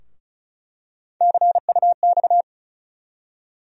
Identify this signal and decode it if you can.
Morse code, heard as a keyed tone, about 36 wpm, 700 Hz. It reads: CUX
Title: USA Call Sign Postfixes - 3 Letters 35wpm